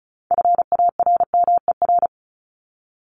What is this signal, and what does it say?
Morse code, heard as a keyed tone, about 34 wpm, 700 Hz. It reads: FARMER
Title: Top 1751-2000 Words 35wpm